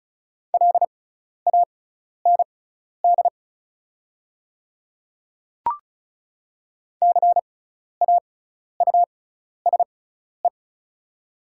Morse code, heard as a keyed tone, about 34 wpm, spaced as for 12 wpm, 690 Hz. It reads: LAND CAUSE
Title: 6x Character Spacing - Top 200 Words - 35wpm